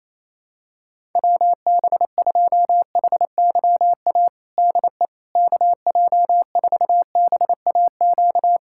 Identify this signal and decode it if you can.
Morse code, heard as a keyed tone, about 28 wpm, 700 Hz. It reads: WB2HYA DE KJ4BAQ
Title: USA Callsign DE Callsign - 6 Characters - 28wpm